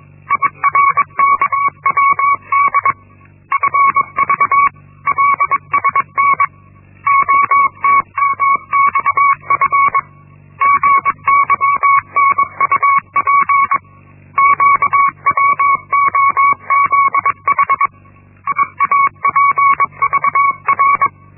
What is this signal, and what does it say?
Morse code, heard as a keyed tone, about 22 wpm, 1120 Hz. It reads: IRKWD FVLSN OTMXF GYNUP QWOZH IAPVR
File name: Linksmakalnis-1